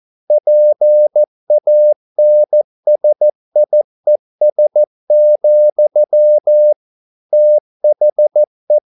Morse code, heard as a keyed tone, about 14 wpm, 600 Hz. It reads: PANSIES, THE
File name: mp3 files / a dogs mistake - 14wpm